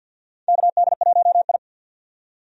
Morse code, heard as a keyed tone, about 51 wpm, 700 Hz. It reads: KD1I
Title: USA Call Signs - 4 Characters - 50wpm